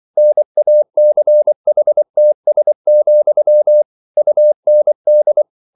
Morse code, heard as a keyed tone, about 24 wpm, 600 Hz. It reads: NACHTS, UND